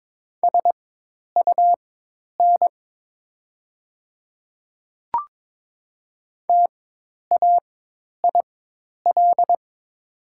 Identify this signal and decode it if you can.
Morse code, heard as a keyed tone, about 22 wpm, spaced as for 10 wpm, 700 Hz. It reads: SUN TAIL
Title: 4x Character Spacing - Top 500 Words - 22wpm